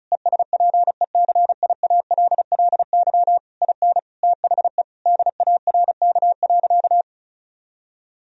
Morse code, heard as a keyed tone, about 34 wpm, 700 Hz. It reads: ESPECIALLY IN THE DARK.
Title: Sentences from Top 1000 Words - Review Part 4 - 35wpm